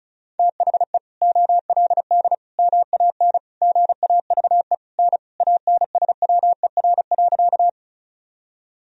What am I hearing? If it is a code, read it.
Morse code, heard as a keyed tone, about 36 wpm, 700 Hz. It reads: THE OLD MAN GAVE N ANSWER.
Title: Sentences from Top 500 Words and Abbreviations 35wpm